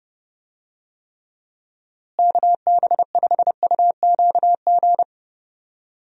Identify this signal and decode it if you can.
Morse code, heard as a keyed tone, about 30 wpm, 700 Hz. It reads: KB5UQG
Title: USA Call Signs - 6 Characters - Repeated 2x - 30wpm